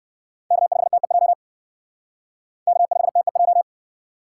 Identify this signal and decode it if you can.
Morse code, heard as a keyed tone, about 56 wpm, 700 Hz. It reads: K5TEJ K5TEJ
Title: Speed Racing - USA 1x3 Call Signs 45wpm